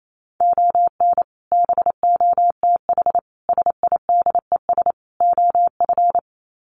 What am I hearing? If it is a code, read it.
Morse code, heard as a keyed tone, about 28 wpm, 700 Hz. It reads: ON BOTH SIDES OF
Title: Sets of 4 Words 28wpm